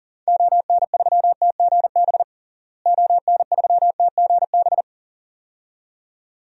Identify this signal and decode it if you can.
Morse code, heard as a keyed tone, about 40 wpm, 700 Hz. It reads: ON3TGB ON3TGB
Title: DXCC-100 Call Signs - 6 Characters - Repeated 2x - 40wpm